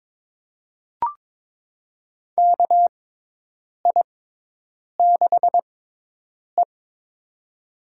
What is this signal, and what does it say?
Morse code, heard as a keyed tone, about 22 wpm, spaced as for 8 wpm, 700 Hz. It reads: KI6E
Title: USA Call Signs - 4 Characters - 6x Character Spacing - 22wpm